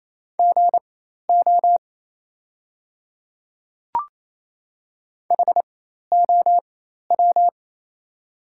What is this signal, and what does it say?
Morse code, heard as a keyed tone, about 28 wpm, spaced as for 13 wpm, 700 Hz. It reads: GO HOW
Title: 4x Character Spacing - Top 200 Words - 28wpm